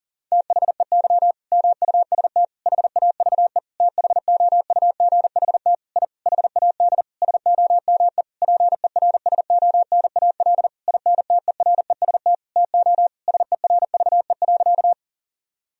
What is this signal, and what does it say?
Morse code, heard as a keyed tone, about 37 wpm, 700 Hz. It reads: THEY MUST HAVE THOUGHT I HAD SOME PERSONAL INTEREST TO SERVE.
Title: Sentences from Top 1000 Words - Review Part 4 - 40wpm